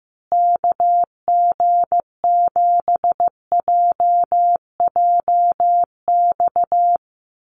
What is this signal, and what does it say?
Morse code, heard as a keyed tone, about 15 wpm, 700 Hz. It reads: KG7JJX
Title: USA Call Signs Repeated 2x 15wpm